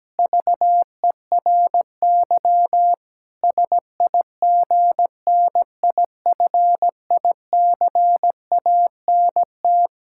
Morse code, heard as a keyed tone, about 17 wpm, 700 Hz. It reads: VERY SIGNIFICANT
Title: Sentences from Top 1000 Words - Review Part 6 - 17wpm